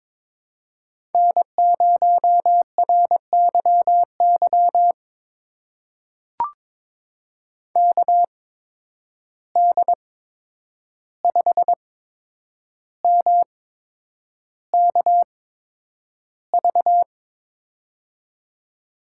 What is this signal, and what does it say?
Morse code, heard as a keyed tone, about 22 wpm, spaced as for 9 wpm, 700 Hz. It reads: N0RYY KD5MKV
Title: USA Call Signs - 8x Character Spacing - 22wpm